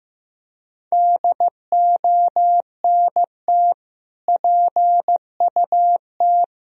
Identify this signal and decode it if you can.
Morse code, heard as a keyed tone, about 16 wpm, 700 Hz. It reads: DONT PUT
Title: Sentences from Top 1000 Words - Review Part 2 - 15wpm